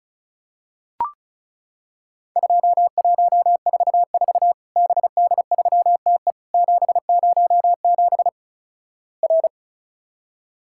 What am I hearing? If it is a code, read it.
Morse code, heard as a keyed tone, about 36 wpm, 700 Hz. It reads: 2144 BD3TE 707 R
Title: Worked All Europe DX Contest - QTC Exchange - 35wpm